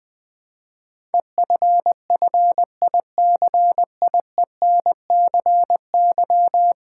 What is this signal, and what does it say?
Morse code, heard as a keyed tone, about 20 wpm, 700 Hz. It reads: EFFICIENCY
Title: Build A Word - 4 Letters At A Time - 10-Letter Words - 20wpm